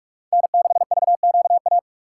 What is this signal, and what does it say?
Morse code, heard as a keyed tone, about 58 wpm, 700 Hz. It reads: N6VQA